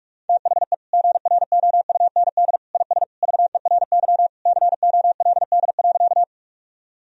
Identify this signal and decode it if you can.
Morse code, heard as a keyed tone, about 46 wpm, 700 Hz. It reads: THE GROUND IS VERY COLD.
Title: Sentences from Top 1000 Words - Review Part 2 - 45wpm